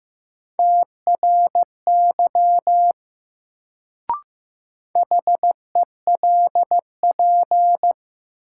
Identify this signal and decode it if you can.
Morse code, heard as a keyed tone, about 15 wpm, 700 Hz. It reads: TRY HELP
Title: Sentences from Top 200 Words - New Word Drill - 15wpm